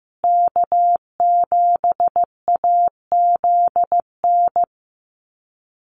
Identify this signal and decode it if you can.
Morse code, heard as a keyed tone, about 15 wpm, 700 Hz. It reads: K7AZN
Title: USA Call Signs Repeated 2x 15wpm